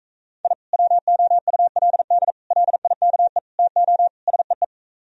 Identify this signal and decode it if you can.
Morse code, heard as a keyed tone, about 45 wpm, 700 Hz. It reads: I WOULD LIKE TO SEE